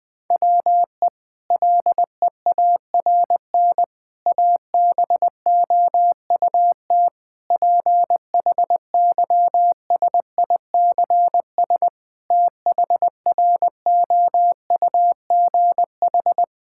Morse code, heard as a keyed tone, about 20 wpm, 700 Hz. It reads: WE LEARN ABOUT PHYSICS THROUGH
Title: Sentences from Top 1000 Words - Review Part 5 - 20wpm